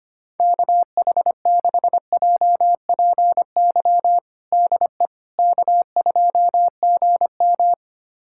Morse code, heard as a keyed tone, about 25 wpm, 700 Hz. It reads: KH6JPY DE K2GM
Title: USA Callsign DE Callsign 25wpm